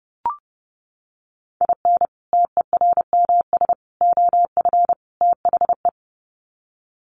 Morse code, heard as a keyed tone, about 30 wpm, 700 Hz. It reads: IN TERMS OF THE
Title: Sets of 4 Words 30wpm